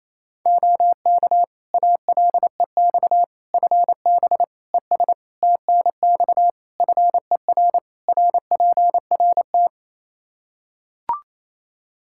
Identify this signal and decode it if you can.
Morse code, heard as a keyed tone, about 28 wpm, 700 Hz. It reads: OK ALEX FB ES TNX FER RPRT
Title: Practice QSO - Exchange 2 - Info - 28wpm